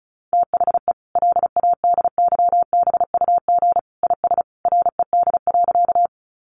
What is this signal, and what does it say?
Morse code, heard as a keyed tone, about 34 wpm, 700 Hz. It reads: THE LADYBUG IS RED.
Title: Additional Easy Phrases 35wpm